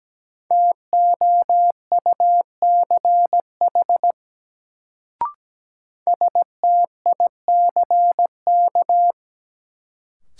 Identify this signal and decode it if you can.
Morse code, heard as a keyed tone, about 17 wpm, 700 Hz. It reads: TOUCH STICK
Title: Sentences from Top 1200 Words - New Word Drill - 17wpm